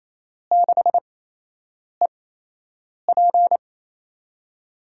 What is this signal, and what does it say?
Morse code, heard as a keyed tone, about 28 wpm, spaced as for 8 wpm, 700 Hz. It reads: 6EP
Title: USA Call Signs - 4 Characters - 8x Character Spacing - 28wpm